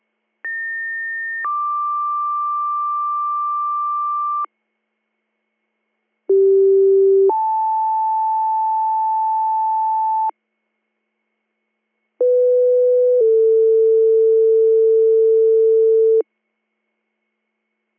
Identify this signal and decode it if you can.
Two-tone page, heard as a one-second tone, then a three-second tone, about 1800 then 1180 Hz, about 390 then 880 Hz, about 500 then 450 Hz. It is heard from 0.4 to 16.2 s.